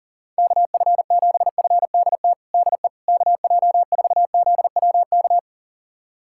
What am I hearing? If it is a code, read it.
Morse code, heard as a keyed tone, about 37 wpm, 700 Hz. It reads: KF7FDT DE KJ4ZWK
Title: USA Callsign DE Callsign - 6 Characters - 40wpm